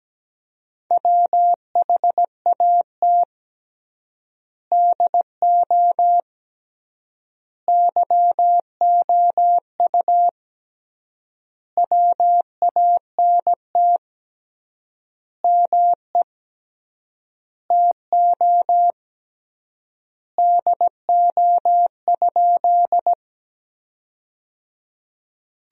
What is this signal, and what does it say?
Morse code, heard as a keyed tone, about 17 wpm, 700 Hz. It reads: WHAT DO YOU WANT ME TO DO?
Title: 3x Word-Spacing - Sentences from Top 200 Words - 17wpm